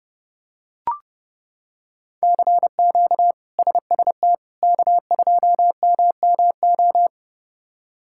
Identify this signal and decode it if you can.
Morse code, heard as a keyed tone, about 32 wpm, 700 Hz. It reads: CQ SST K2MMO
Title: K1USN SST QSO Practice - 30wpm